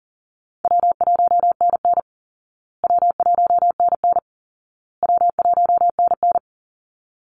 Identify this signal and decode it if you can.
Morse code, heard as a keyed tone, about 40 wpm, 700 Hz. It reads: W1NN W1NN W1NN
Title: Easy Call Signs Repeated 3x - 40wpm